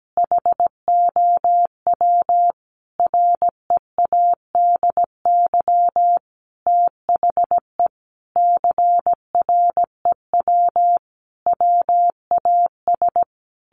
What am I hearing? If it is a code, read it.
Morse code, heard as a keyed tone, about 17 wpm, 700 Hz. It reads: HOW READY THE CREW WAS
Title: Sentences from Top 1500 Words 17wpm